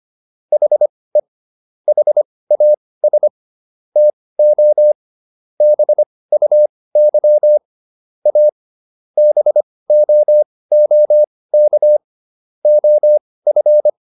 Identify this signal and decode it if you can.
Morse code, heard as a keyed tone, about 25 wpm, spaced as for 18 wpm, 600 Hz. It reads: HE HAS TO BUY A BOOK OF